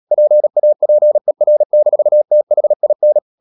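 Morse code, heard as a keyed tone, about 37 wpm, 600 Hz. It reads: PAPER-THIN
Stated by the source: Morse code practice at 35–39 wpm sending English text of Ken Follett's novel Never, from CW med Gnister.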